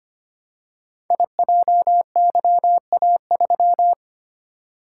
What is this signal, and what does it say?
Morse code, heard as a keyed tone, about 25 wpm, 700 Hz. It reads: IJYA3
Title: Code Groups - 5 Letter:Numbers - 25wpm